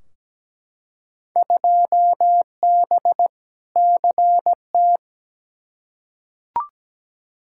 Morse code, heard as a keyed tone, about 17 wpm, 700 Hz. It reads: 2B CT
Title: Field Day Exchanges Encore - 17wpm